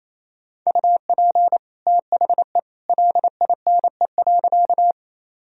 Morse code, heard as a keyed tone, about 28 wpm, 700 Hz. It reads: UP THE LINE.
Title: Sentences from Top 1000 Words - Review Part 1 - 28wpm